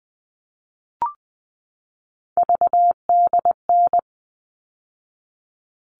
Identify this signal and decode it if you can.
Morse code, heard as a keyed tone, about 20 wpm, 700 Hz. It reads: VDN
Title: All Cut Numbers - 3-Digit 20wpm